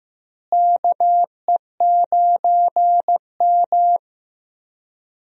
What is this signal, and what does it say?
Morse code, heard as a keyed tone, about 15 wpm, 700 Hz. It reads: KE9M
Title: USA Call Signs - 4 Characters - Repeated 3x - 15wpm